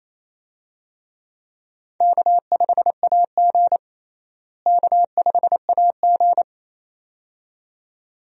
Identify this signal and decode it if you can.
Morse code, heard as a keyed tone, about 28 wpm, 700 Hz. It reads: K5AG K5AG
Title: USA Call Signs - 4 Characters - Repeated 2x - 28wpm